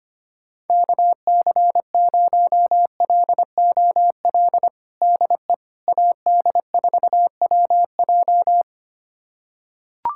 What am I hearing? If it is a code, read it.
Morse code, heard as a keyed tone, about 25 wpm, 700 Hz. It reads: KC0LOL DE AD4WJ
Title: USA Callsign DE Callsign 25wpm